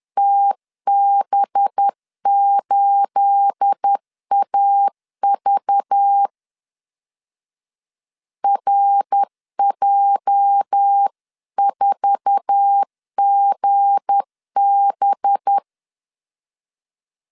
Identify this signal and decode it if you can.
Morse code, heard as a keyed tone, about 10 wpm, 800 Hz. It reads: TB8AV RJ4GB
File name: CW / DET / 1 9 1500C S B 10wpm 4439